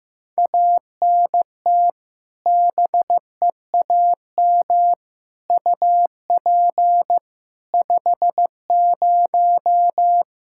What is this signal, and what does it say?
Morse code, heard as a keyed tone, about 15 wpm, 700 Hz. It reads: ANT BEAM UP 50
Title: Practice QSO - Exchange 2 - Info - 15wpm